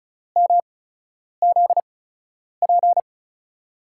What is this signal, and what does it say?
Morse code, heard as a keyed tone, about 36 wpm, spaced as for 10 wpm, 700 Hz. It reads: MZP
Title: USA Call Signs - 6 Characters - 8x Character Spacing - 35wpm